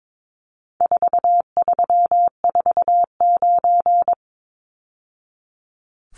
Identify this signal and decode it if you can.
Morse code, heard as a keyed tone, about 22 wpm, 700 Hz. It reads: 4349
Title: Numbers - 4-Digit 22wpm